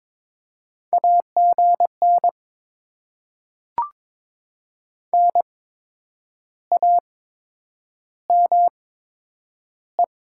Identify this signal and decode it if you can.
Morse code, heard as a keyed tone, about 22 wpm, spaced as for 9 wpm, 700 Hz. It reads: AGN NAME
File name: Top 20 QSO Elements - 8x Character Spacing - 22wpm